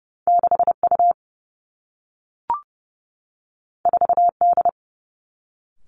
Morse code, heard as a keyed tone, about 30 wpm, 700 Hz. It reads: 6 U 4 D
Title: Two Challenging Letter-Number Combinations 30wpm